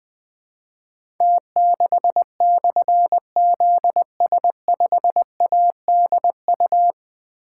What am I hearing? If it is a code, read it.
Morse code, heard as a keyed tone, about 20 wpm, 700 Hz. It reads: T6/ZS5ADU
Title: Difficult Call Signs 20wpm